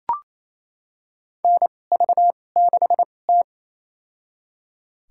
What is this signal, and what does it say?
Morse code, heard as a keyed tone, about 28 wpm, spaced as for 20 wpm, 700 Hz. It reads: NV6T
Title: USA Call Signs - 4 Characters - 2x Character Spacing - 28wpm